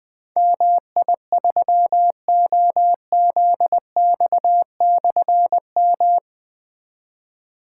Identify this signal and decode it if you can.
Morse code, heard as a keyed tone, about 20 wpm, 700 Hz. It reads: MI3OZX/M
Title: Difficult Call Signs 20wpm